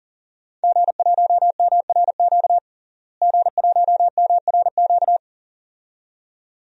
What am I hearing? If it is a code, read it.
Morse code, heard as a keyed tone, about 40 wpm, 700 Hz. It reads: G1MRQ G1MRQ
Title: DXCC-100 Call Signs - 5 Characters - Repeated 2x - 40wpm